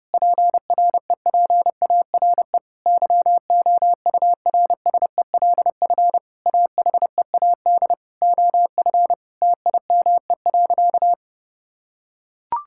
Morse code, heard as a keyed tone, about 30 wpm, 700 Hz. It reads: PREPARE YOURSELF AHEAD OF TIME.
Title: Sentences from Top 1000 Words - Review Part 6 - 30wpm